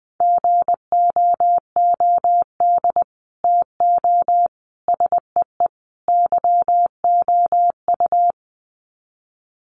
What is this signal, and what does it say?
Morse code, heard as a keyed tone, about 20 wpm, 700 Hz. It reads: GOOD TO SEE YOU
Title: Sets of 4 Words 20wpm